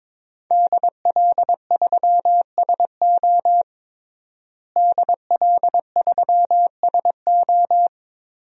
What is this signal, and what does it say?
Morse code, heard as a keyed tone, about 21 wpm, 700 Hz. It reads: DL3SO DL3SO
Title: DXCC-100 Call Signs - 5 Characters - Repeated 3x - 22wpm